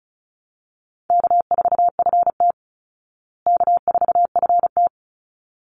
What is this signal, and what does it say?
Morse code, heard as a keyed tone, about 34 wpm, 700 Hz. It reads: K4FT K4FT
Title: Easy Call Signs Repeated 2x - 35wpm